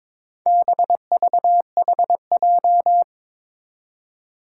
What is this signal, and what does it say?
Morse code, heard as a keyed tone, about 22 wpm, 700 Hz. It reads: BVHJ
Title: Four Challenging Letter-Number Combinations 22wpm